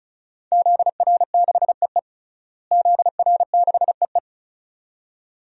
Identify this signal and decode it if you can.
Morse code, heard as a keyed tone, about 36 wpm, 700 Hz. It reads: ZR6EE ZR6EE
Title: DXCC-100 Call Signs - 5 Characters - Repeated 2x - 35wpm